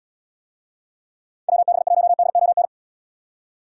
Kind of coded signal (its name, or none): Morse code